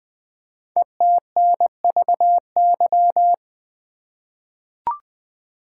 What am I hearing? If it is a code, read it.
Morse code, heard as a keyed tone, about 20 wpm, 700 Hz. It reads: ETNVY
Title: Code Groups - 5 Letters - 20wpm